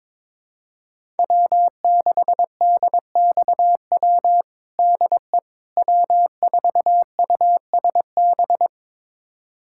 Morse code, heard as a keyed tone, about 22 wpm, 700 Hz. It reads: W6DXW DE W4USB
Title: USA Callsign DE Callsign - 5 Characters - 22wpm